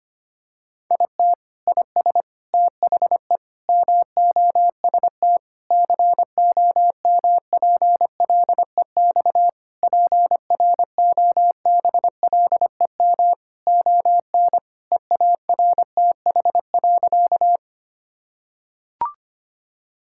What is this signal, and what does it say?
Morse code, heard as a keyed tone, about 25 wpm, 700 Hz. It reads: IT IS THE MOST COMPLEX PROBLEM ON EARTH.
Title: Sentences from Top 1000 Words - Review Part 6 - 25wpm